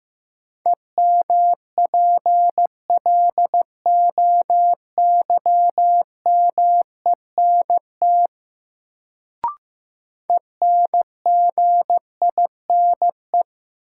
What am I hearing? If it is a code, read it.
Morse code, heard as a keyed tone, about 15 wpm, 700 Hz. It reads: EMPLOYMENT ENGINE
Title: Sentences from Top 1500 Words - New Word Drill Alt - 15wpm